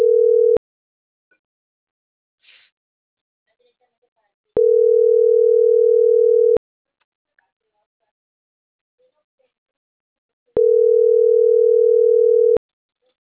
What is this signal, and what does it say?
Ringback tone (North America), 440 + 480 Hz.